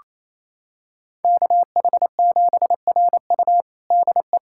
Morse code, heard as a keyed tone, about 28 wpm, 700 Hz. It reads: KH7RU DE
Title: USA Callsign DE Callsign - 5 Characters - 28wpm